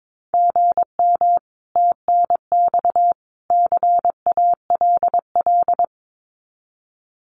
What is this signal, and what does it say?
Morse code, heard as a keyed tone, about 22 wpm, 700 Hz. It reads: GM TNX CALL
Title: Common QSO Phrases 22wpm